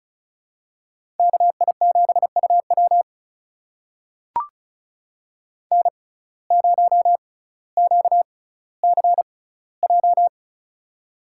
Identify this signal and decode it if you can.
Morse code, heard as a keyed tone, about 36 wpm, spaced as for 18 wpm, 700 Hz. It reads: KI7UW N0QCJ
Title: USA Call Signs - 5 Characters - 6x Character Spacing - 35wpm